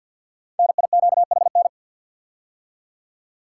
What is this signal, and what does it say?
Morse code, heard as a keyed tone, about 46 wpm, 700 Hz. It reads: NIQHN
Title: Code Groups - 5 Letter:Numbers - 50wpm